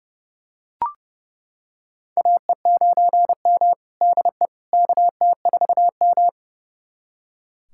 Morse code, heard as a keyed tone, about 30 wpm, 700 Hz. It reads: AE9M DE KT4M
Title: USA Callsign DE Callsign - 4 Characters - 30wpm